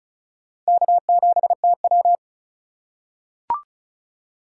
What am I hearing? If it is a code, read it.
Morse code, heard as a keyed tone, about 34 wpm, 700 Hz. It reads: K7TW
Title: USA Call Signs - 4 Characters - 3x Character Spacing - 35wpm